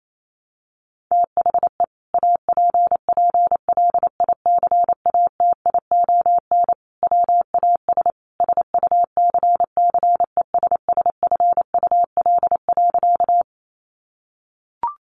Morse code, heard as a keyed tone, about 28 wpm, 700 Hz. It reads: THE APPLICATION WAS SUCCESSFUL.
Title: Sentences from Top 1200 Words 28wpm